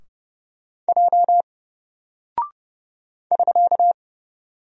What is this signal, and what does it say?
Morse code, heard as a keyed tone, about 31 wpm, 700 Hz. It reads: J <SK>
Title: Single Character (Letter:Number:Punctuation:Prosign) 30wpm